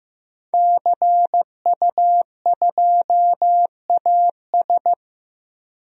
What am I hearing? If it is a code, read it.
Morse code, heard as a keyed tone, about 15 wpm, 700 Hz. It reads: CU2AS
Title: DXCC-100 Call Signs - 5 Characters - 15wpm